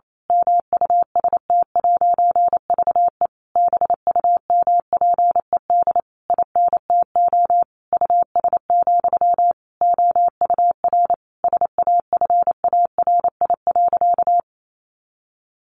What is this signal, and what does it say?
Morse code, heard as a keyed tone, about 28 wpm, 700 Hz. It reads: MUST'VE BUMPED INTO US, OUR SAFARI.
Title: A Sound Of Thunder 28wpm